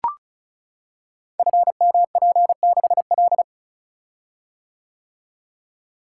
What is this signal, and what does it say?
Morse code, heard as a keyed tone, about 34 wpm, 700 Hz. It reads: FMP6L
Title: Code Groups - 5 Letter:Numbers - 35wpm